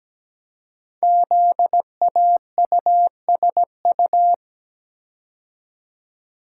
Morse code, heard as a keyed tone, about 17 wpm, 700 Hz. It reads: ZAUSU
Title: Code Groups - 5 Letters - 17wpm